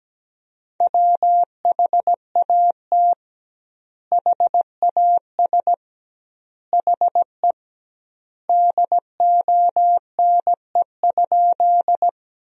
Morse code, heard as a keyed tone, about 17 wpm, 700 Hz. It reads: WHAT HAS HE DONE?
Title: 2x Word-Spacing - Sentences from Top 500 Words - 17wpm